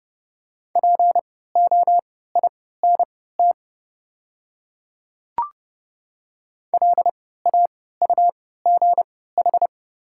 Morse code, heard as a keyed tone, about 30 wpm, spaced as for 17 wpm, 700 Hz. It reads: POINT LAUGH
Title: 3x Character Spacing - Top 500 Words - 30wpm